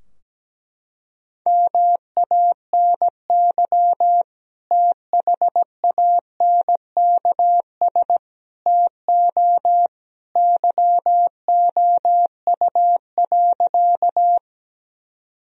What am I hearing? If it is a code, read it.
Morse code, heard as a keyed tone, about 17 wpm, 700 Hz. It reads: MANY THANKS TO YOU.
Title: Sentences from Top 1000 Words - Review Part 3 - 17wpm